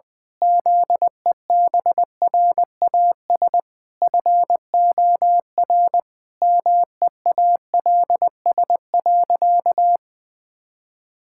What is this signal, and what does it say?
Morse code, heard as a keyed tone, about 20 wpm, 700 Hz. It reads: ZEBRAS FOR MEALS.
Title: Pangram Challenge v2 - 20wpm